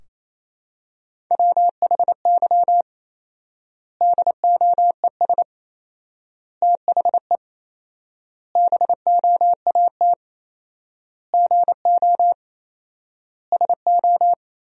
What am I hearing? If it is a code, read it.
Morse code, heard as a keyed tone, about 28 wpm, 700 Hz. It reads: WHY DOES THE BOAT GO SO
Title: 4x Word-Spacing - Sentences from Top 500 Words - 28wpm